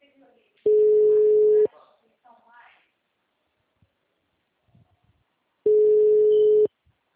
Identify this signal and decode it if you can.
Ringback tone (Europe, 425 Hz), with speech.
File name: out- -2016- - - .92828